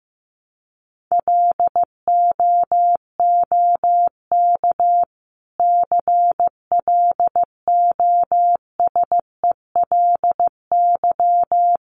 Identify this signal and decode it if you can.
Morse code, heard as a keyed tone, about 15 wpm, 700 Hz. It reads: LOOK CLOSELY
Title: Sentences from Top 1400 Words 15wpm